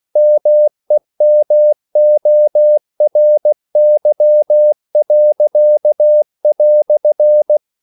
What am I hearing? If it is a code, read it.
Morse code, heard as a keyed tone, about 16 wpm, 600 Hz. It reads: MEMORY."